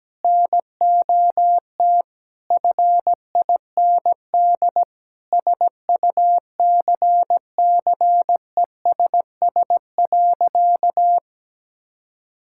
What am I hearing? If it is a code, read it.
Morse code, heard as a keyed tone, about 17 wpm, 700 Hz. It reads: NOT FIND SUCCESS.
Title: Sentences from Top 1000 Words - Review Part 5 - 17wpm